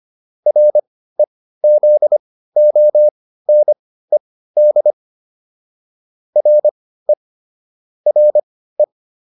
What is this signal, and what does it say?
Morse code, heard as a keyed tone, about 25 wpm, spaced as for 15 wpm, 600 Hz. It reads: REZONED RE RE